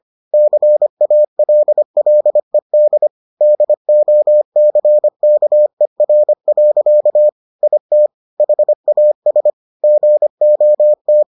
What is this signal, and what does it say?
Morse code, heard as a keyed tone, about 25 wpm, 600 Hz. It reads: CALLED DOCKER. IT HAS GOT